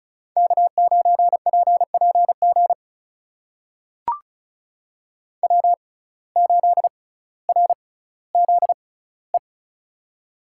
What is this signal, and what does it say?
Morse code, heard as a keyed tone, about 36 wpm, spaced as for 18 wpm, 700 Hz. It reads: K9PPG W8RZE
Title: USA Call Signs - 5 Characters - 6x Character Spacing - 35wpm